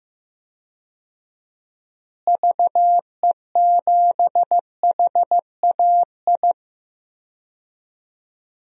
Morse code, heard as a keyed tone, about 15 wpm, 700 Hz. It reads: VE7HAI
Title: DXCC-100 Call Signs - 6 Characters - Repeated 2x - 15wpm